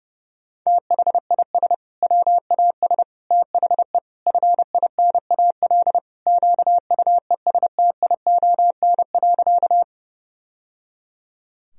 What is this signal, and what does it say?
Morse code, heard as a keyed tone, about 32 wpm, 700 Hz. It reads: THIS WAS THE FINAL QUESTION.
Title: Sentences from Top 1000 Words - Review Part 2 - 30wpm